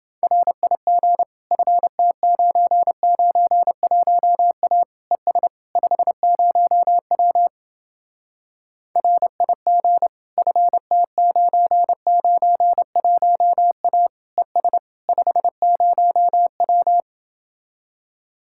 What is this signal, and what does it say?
Morse code, heard as a keyed tone, about 29 wpm, 700 Hz. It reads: RIG FT991A ES 50W RIG FT991A ES 50W